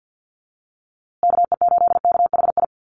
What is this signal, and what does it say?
Morse code, heard as a keyed tone, about 50 wpm, 700 Hz. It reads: KE8KHI
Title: USA Call Signs Repeated 2x 50wpm